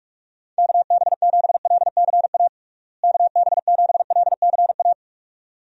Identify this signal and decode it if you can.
Morse code, heard as a keyed tone, about 46 wpm, 700 Hz. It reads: KB7LCA KB7LCA
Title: USA Call Signs - 6 Characters - Repeated 2x - 45wpm